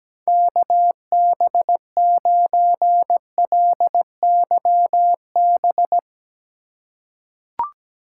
Morse code, heard as a keyed tone, about 17 wpm, 700 Hz. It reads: KB9LYB
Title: USA Call Signs - 6 Characters - 3x Character Spacing - 17wpm